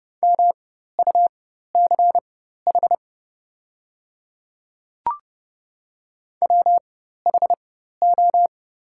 Morse code, heard as a keyed tone, about 30 wpm, spaced as for 14 wpm, 700 Hz. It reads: MUCH WHO
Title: 4x Character Spacing - Top 300 Words - 30wpm